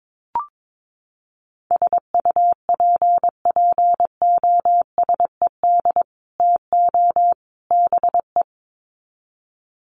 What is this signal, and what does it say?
Morse code, heard as a keyed tone, about 22 wpm, 700 Hz. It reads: SUPPOSED TO BE
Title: Sets of 3 Words 22wpm